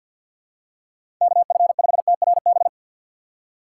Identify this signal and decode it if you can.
Morse code, heard as a keyed tone, about 53 wpm, 700 Hz. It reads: KF5TRB
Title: USA Call Signs - 6 Characters - 4x Character Spacing - 50wpm